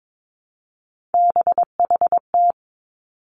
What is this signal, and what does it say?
Morse code, heard as a keyed tone, about 22 wpm, 700 Hz. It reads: BHT
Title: Three Challenging Letter-Number Combinations 22wpm